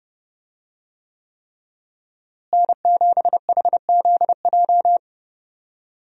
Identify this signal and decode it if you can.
Morse code, heard as a keyed tone, about 30 wpm, 700 Hz. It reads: N7HZJ
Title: USA Call Signs - 5 Characters - Repeated 3x - 30wpm